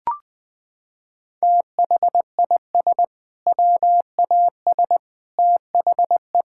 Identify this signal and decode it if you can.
Morse code, heard as a keyed tone, about 21 wpm, 700 Hz. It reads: THIS WAS THE
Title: Sentences from Top 1000 Words - Review Part 6 - 20wpm